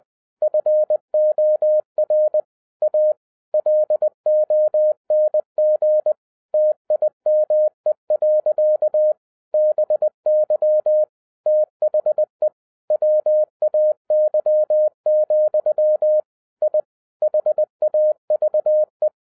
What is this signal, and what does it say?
Morse code, heard as a keyed tone, about 20 wpm, 610 Hz. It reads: FOR A LONG TIME. BY THE WAY, I HAVE